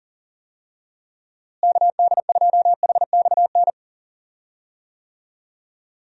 Morse code, heard as a keyed tone, about 40 wpm, 700 Hz. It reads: KD2HXN
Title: USA Call Signs - 6 Characters - Repeated 3x - 40wpm